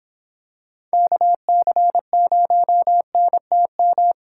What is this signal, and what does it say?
Morse code, heard as a keyed tone, about 25 wpm, 700 Hz. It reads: KC0NTM